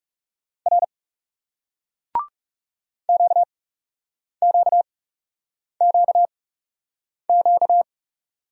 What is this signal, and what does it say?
Morse code, heard as a keyed tone, about 39 wpm, spaced as for 8 wpm, 700 Hz. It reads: R QQQQ